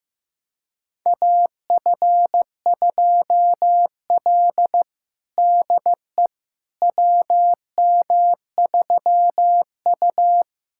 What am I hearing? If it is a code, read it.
Morse code, heard as a keyed tone, about 15 wpm, 700 Hz. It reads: AF2L DE WM3U
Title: USA Callsign DE Callsign - 4 Characters - 15wpm